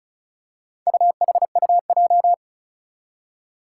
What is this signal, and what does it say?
Morse code, heard as a keyed tone, about 36 wpm, 700 Hz. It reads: UHUJ
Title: Four Challenging Letter-Number Combinations 35wpm